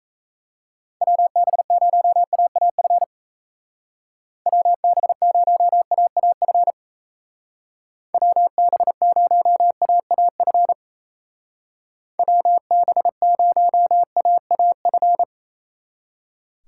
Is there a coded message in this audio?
Yes: Morse code from 1.0 to 15.2 s, heard as a keyed tone, about 37 wpm, 700 Hz. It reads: WB0AAF WB0AAF WB0AAF WB0AAF